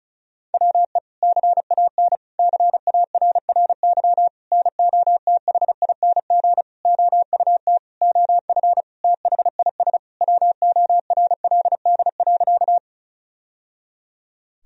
Morse code, heard as a keyed tone, about 36 wpm, 700 Hz. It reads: WE CAN CARRY NOTHING OUT OF THIS WORLD.
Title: Sentences from Top 1000 Words - Review Part 1 - 35wpm